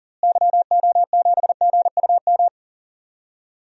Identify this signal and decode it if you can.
Morse code, heard as a keyed tone, about 40 wpm, 700 Hz. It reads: YO7GUM
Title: DXCC-100 Call Signs - 6 Characters - Repeated 2x - 40wpm